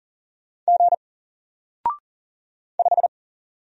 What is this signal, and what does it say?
Morse code, heard as a keyed tone, about 40 wpm, 700 Hz. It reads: G 5
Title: Single Character (Letter:Number:Punctuation:Prosign) Alt 40wpm